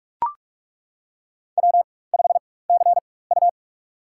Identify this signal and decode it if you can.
Morse code, heard as a keyed tone, about 42 wpm, spaced as for 20 wpm, 700 Hz. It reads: W5CU
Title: USA Call Signs - 4 Characters - 4x Character Spacing - 45wpm